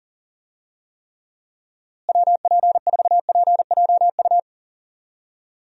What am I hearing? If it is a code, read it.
Morse code, heard as a keyed tone, about 40 wpm, 700 Hz. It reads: WP4PJU
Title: USA Call Signs - 6 Characters - Repeated 2x - 40wpm